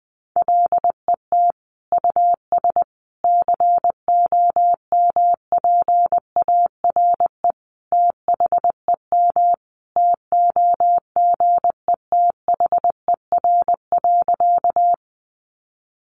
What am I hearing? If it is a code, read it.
Morse code, heard as a keyed tone, about 20 wpm, 700 Hz. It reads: LET US COMPARE THEM TOGETHER.
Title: Sentences from Top 1300 Words 20wpm